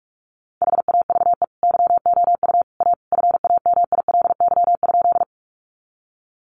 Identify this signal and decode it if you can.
Morse code, heard as a keyed tone, about 44 wpm, 700 Hz. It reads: HAVE YOU A FAMILY?
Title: Sentences from Top 400 Words 45wpm